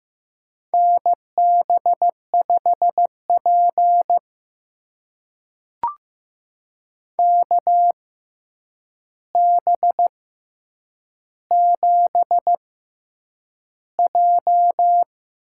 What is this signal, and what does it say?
Morse code, heard as a keyed tone, about 15 wpm, spaced as for 8 wpm, 700 Hz. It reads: NB5P KB7J
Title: USA Call Signs - 4 Characters - 6x Character Spacing - 15wpm